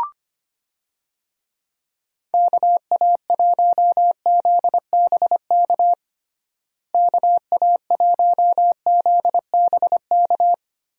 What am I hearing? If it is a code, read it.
Morse code, heard as a keyed tone, about 25 wpm, 700 Hz. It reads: KA1ZBK KA1ZBK
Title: USA Call Signs - 6 Characters - Repeated 3x - 25wpm